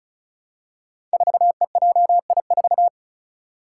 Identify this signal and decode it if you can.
Morse code, heard as a keyed tone, about 34 wpm, 700 Hz. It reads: 4EJI4
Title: Five Challenging Letter-Number Combinations 35wpm